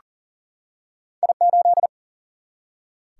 Morse code, heard as a keyed tone, about 39 wpm, 700 Hz. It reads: I 8
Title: Single-Letter Single-Number 40wpm